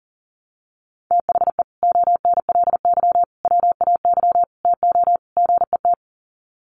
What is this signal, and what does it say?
Morse code, heard as a keyed tone, about 40 wpm, 700 Hz. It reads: THE ONLY WAY TO GET
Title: Sets of 5 Words Encore2 40wpm